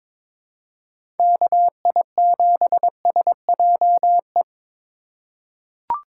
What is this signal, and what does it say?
Morse code, heard as a keyed tone, about 21 wpm, 700 Hz. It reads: KI7SJE
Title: USA Call Signs - 6 Characters - 4x Character Spacing - 22wpm